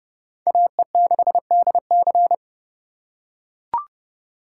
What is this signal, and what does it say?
Morse code, heard as a keyed tone, about 27 wpm, 700 Hz. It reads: AE6DC